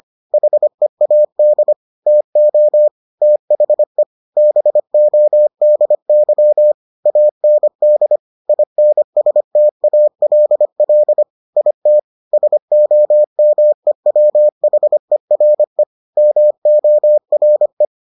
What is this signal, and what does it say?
Morse code, heard as a keyed tone, about 25 wpm, 600 Hz. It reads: HEAD TO THE BODY AND INSTALL IT SOMEWHERE MORE